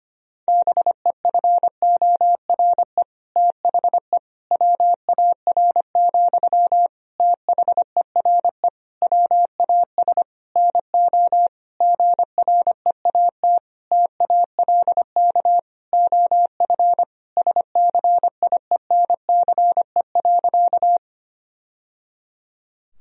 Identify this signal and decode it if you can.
Morse code, heard as a keyed tone, about 25 wpm, 700 Hz. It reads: BEFORE THE WAR, THERE WAS NO GREAT TALK OF SCIENCE.
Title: Sentences from Top 1000 Words - Review Part 2 - 25wpm